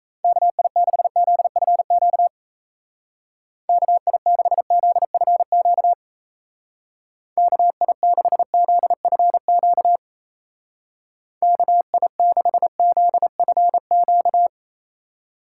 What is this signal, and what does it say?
Morse code, heard as a keyed tone, about 36 wpm, 700 Hz. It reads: KI6ZFQ KI6ZFQ KI6ZFQ KI6ZFQ